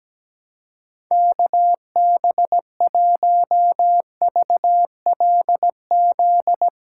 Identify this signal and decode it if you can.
Morse code, heard as a keyed tone, about 17 wpm, 700 Hz. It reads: KB1VLZ
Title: USA Callsign DE Callsign 17wpm